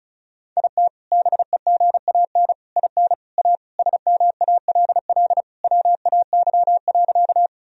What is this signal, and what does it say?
Morse code, heard as a keyed tone, about 36 wpm, 700 Hz. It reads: IT BEGAN IN A SMALL WAY.
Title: Sentences from Top 1000 Words - Review Part 1 - 35wpm